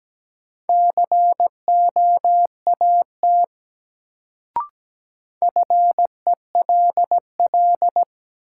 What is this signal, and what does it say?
Morse code, heard as a keyed tone, about 17 wpm, 700 Hz. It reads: COAT FELL
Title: Sentences from Top 1600 Words - New Word Drill Alt - 17wpm